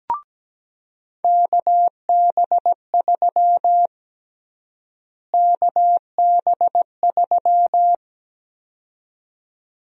Morse code, heard as a keyed tone, about 17 wpm, 700 Hz. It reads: KB3 KB3
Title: USA Call Sign Prefixes - Repeated 2x - 17wpm